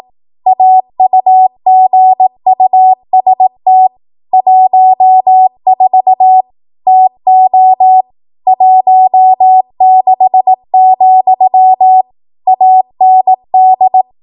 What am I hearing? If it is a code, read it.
Morse code, heard as a keyed tone, about 18 wpm, 750 Hz. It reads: AUGUST 14 TO 16, AND
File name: Archive / 18 WPM / 18WPM